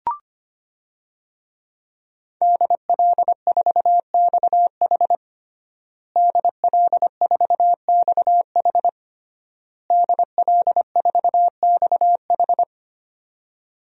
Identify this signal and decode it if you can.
Morse code, heard as a keyed tone, about 25 wpm, 700 Hz. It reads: DL4XH DL4XH DL4XH
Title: DXCC-100 Call Signs - 5 Characters - Repeated 3x - 25wpm